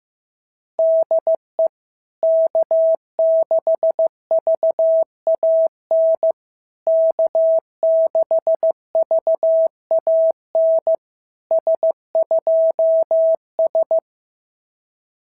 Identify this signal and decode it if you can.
Morse code, heard as a keyed tone, about 14 wpm, 650 Hz. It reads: DE K6VAN K6VAN S2S
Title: SOTA QSO Practice 15wpm